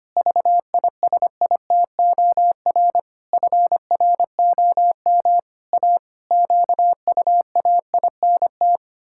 Morse code, heard as a keyed tone, about 25 wpm, 700 Hz. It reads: VISITOR FROM A QUAINT
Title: Pangram Challenge v2 - 25wpm